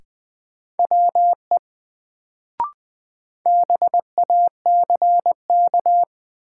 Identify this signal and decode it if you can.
Morse code, heard as a keyed tone, about 20 wpm, 700 Hz. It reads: WE BACK
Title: Sets of 2 Words - New Word Drill - 20wpm